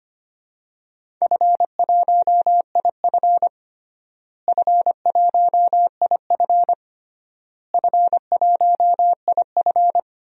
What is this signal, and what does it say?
Morse code, heard as a keyed tone, about 25 wpm, 700 Hz. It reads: F1IF F1IF F1IF
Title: DXCC-100 Call Signs - 4 Characters - Repeated 3x - 25wpm